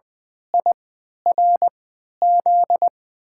Morse code, heard as a keyed tone, about 20 wpm, spaced as for 11 wpm, 700 Hz. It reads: IRZ
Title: USA Call Signs - 5 Characters - 3x Character Spacing - 20wpm